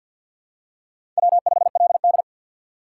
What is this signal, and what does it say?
Morse code, heard as a keyed tone, about 52 wpm, 700 Hz. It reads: W5LD
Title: USA Call Signs - 4 Characters - 2x Character Spacing - 50wpm